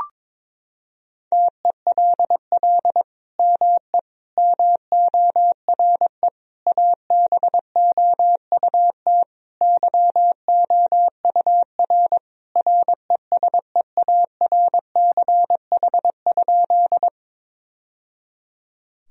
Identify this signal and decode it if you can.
Morse code, heard as a keyed tone, about 21 wpm, 700 Hz. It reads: TELL ME MORE ABOUT YOUR RESEARCH?
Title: Sentences from Top 1000 Words - Review Part 4 - 22wpm